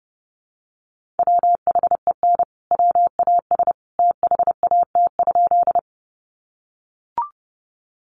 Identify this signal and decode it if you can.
Morse code, heard as a keyed tone, about 30 wpm, 700 Hz. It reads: WHEN WAS THAT?
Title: Sentences from Top 100 Words 30wpm